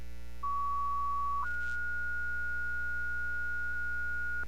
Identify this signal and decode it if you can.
Two-tone page, heard as a one-second tone, then a three-second tone, about 1130 then 1440 Hz, with noise. It is heard from 0.4 to 4.4 s.